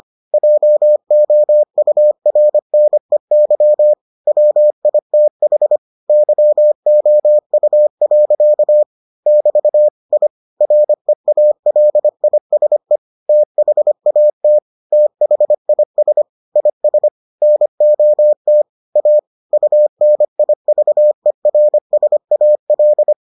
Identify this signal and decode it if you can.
Morse code, heard as a keyed tone, about 25 wpm, 600 Hz. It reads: JOURNEY WITH YOU. <BT> I REALISE THAT THIS IS NOT A UNIVERSAL